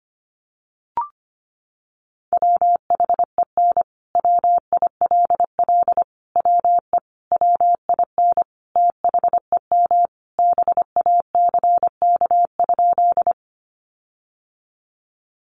Morse code, heard as a keyed tone, about 25 wpm, 700 Hz. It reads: WHEN WILL WE WIN THEM BACK?
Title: Sentences from Top 1500 Words 25wpm